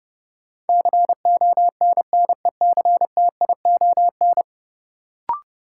Morse code, heard as a keyed tone, about 30 wpm, 700 Hz. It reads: CONNECTION
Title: Sentences from Top 1300 Words - New Word Drill - 30wpm